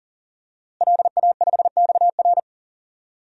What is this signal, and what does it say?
Morse code, heard as a keyed tone, about 40 wpm, 700 Hz. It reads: LA5XR
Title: DXCC-100 Call Signs - 5 Characters - Repeated 3x - 40wpm